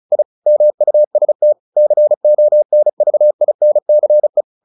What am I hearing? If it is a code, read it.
Morse code, heard as a keyed tone, about 34 wpm, 600 Hz. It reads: I MUST CONVINCE
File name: the old man and the sea / the old man and the sea 1 35wpm